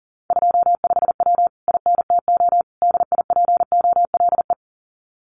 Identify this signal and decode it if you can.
Morse code, heard as a keyed tone, about 40 wpm, 700 Hz. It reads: 25W INTO DIPOLE
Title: Common QSO Phrases Encore 40wpm